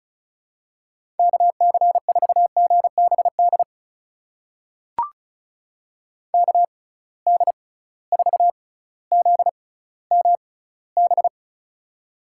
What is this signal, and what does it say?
Morse code, heard as a keyed tone, about 34 wpm, spaced as for 18 wpm, 700 Hz. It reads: KC4GBD KD4ZMB
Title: USA Call Signs - 6 Characters - 6x Character Spacing - 35wpm